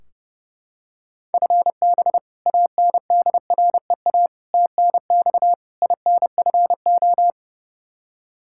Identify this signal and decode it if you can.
Morse code, heard as a keyed tone, about 32 wpm, 700 Hz. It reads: FB ANDREA TNX INFO
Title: Practice QSO - Exchange 3 - More Info - 30wpm